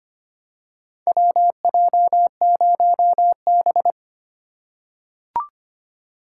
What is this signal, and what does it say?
Morse code, heard as a keyed tone, about 25 wpm, 700 Hz. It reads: WJ0B
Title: USA Call Signs - 4x Character Spacing - 25wpm